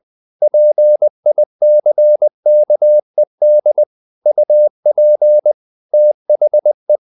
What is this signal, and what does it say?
Morse code, heard as a keyed tone, about 20 wpm, 600 Hz. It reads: PICKED UP THE
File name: the old man and the sea / the old man and the sea part 1 20wpm